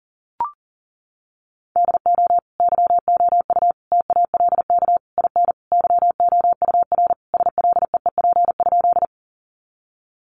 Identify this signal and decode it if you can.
Morse code, heard as a keyed tone, about 40 wpm, 700 Hz. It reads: DO YOU TALK IN YOUR SLEEP?
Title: Sentences from Top 500 Words 40wpm